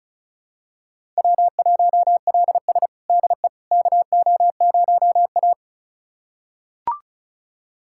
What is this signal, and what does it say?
Morse code, heard as a keyed tone, about 36 wpm, 700 Hz. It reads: W1LS DE KO0A
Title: USA Callsign DE Callsign - 4 Characters - 35wpm